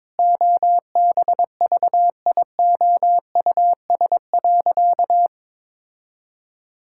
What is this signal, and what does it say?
Morse code, heard as a keyed tone, about 22 wpm, 700 Hz. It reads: OBVIOUS.
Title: Sentences from Top 1600 Words 22wpm